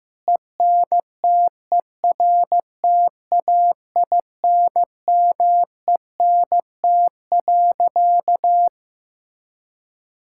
Morse code, heard as a keyed tone, about 15 wpm, 700 Hz. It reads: ENTERTAINMENT.
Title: Sentences from Top 1000 Words - Review Part 6 - 15wpm